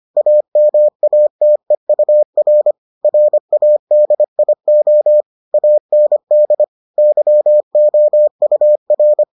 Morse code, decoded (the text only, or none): AMATEUR RADIO AND YOUR